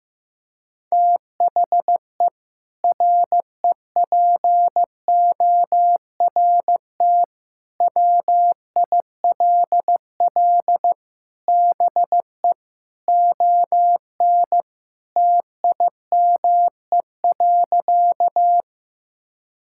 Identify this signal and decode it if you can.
Morse code, heard as a keyed tone, about 15 wpm, 700 Hz. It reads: THE REPORT WILL BE ON TIME.
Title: Sentences from Top 1000 Words - Review Part 6 - 15wpm